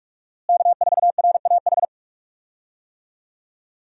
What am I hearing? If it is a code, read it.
Morse code, heard as a keyed tone, about 41 wpm, 700 Hz. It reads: K4RAH
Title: USA Call Signs - 5 Characters - Repeated 3x - 45wpm